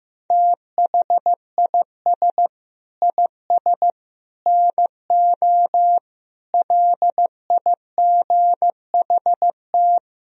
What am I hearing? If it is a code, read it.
Morse code, heard as a keyed tone, about 15 wpm, 700 Hz. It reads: THIS IS NO LIGHT
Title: Sentences from Top 1000 Words - Review Part 1 - 15wpm